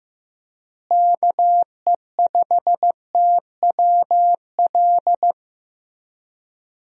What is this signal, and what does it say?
Morse code, heard as a keyed tone, about 16 wpm, 700 Hz. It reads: KE5TWL
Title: USA Call Signs - 6 Characters - Repeated 3x - 15wpm